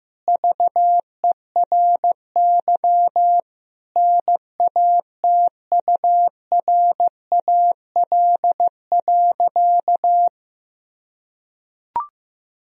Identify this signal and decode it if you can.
Morse code, heard as a keyed tone, about 16 wpm, 700 Hz. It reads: VERY NATURAL.
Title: Sentences from Top 1000 Words - Review Part 4 - 15wpm